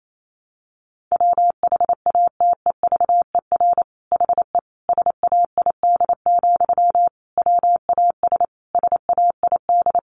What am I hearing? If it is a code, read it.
Morse code, heard as a keyed tone, about 28 wpm, 700 Hz. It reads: WHATEVER HE SAID, WAS SAID
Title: Sentences from Top 1100 Words 28wpm